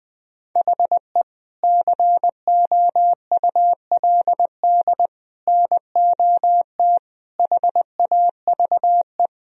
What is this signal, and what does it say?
Morse code, heard as a keyed tone, about 20 wpm, 700 Hz. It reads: HE COULD NOT HAVE
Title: Sentences from Top 1000 Words - Review Part 2 - 20wpm